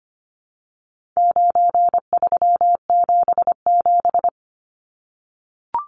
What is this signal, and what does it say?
Morse code, heard as a keyed tone, about 25 wpm, 700 Hz. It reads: 9377
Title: Numbers - 4-Digit 25wpm